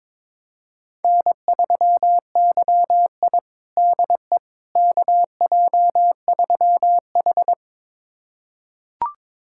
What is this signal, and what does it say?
Morse code, heard as a keyed tone, about 22 wpm, 700 Hz. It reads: N3YI DE KJ3H
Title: USA Callsign DE Callsign - 4 Characters - 22wpm